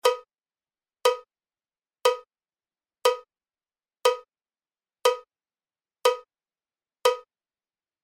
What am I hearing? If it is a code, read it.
There is no code.